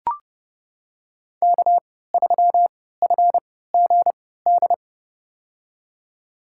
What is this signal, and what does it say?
Morse code, heard as a keyed tone, about 30 wpm, spaced as for 17 wpm, 700 Hz. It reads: K3FGD
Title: USA Call Signs - 3x Character Spacing - 30wpm